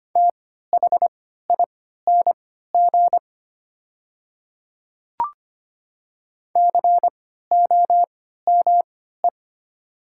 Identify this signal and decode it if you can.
Morse code, heard as a keyed tone, about 25 wpm, spaced as for 14 wpm, 700 Hz. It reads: THING COME
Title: 3x Character Spacing - Top 100 Words - 25wpm